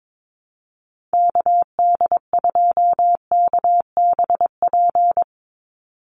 Morse code, heard as a keyed tone, about 22 wpm, 700 Hz. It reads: KD2KBP
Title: USA Call Signs 22wpm